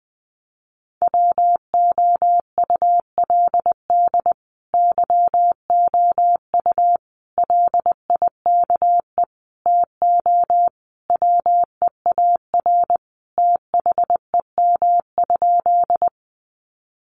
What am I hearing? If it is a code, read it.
Morse code, heard as a keyed tone, about 20 wpm, 700 Hz. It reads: WOULD YOU LIKE TO WEAR THEM?
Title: Sentences from Top 1500 Words 20wpm